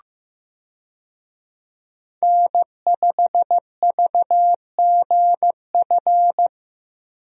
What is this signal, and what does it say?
Morse code, heard as a keyed tone, about 15 wpm, 700 Hz. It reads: N5VGF
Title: USA Call Signs - 5 Characters - Repeated 3x - 15wpm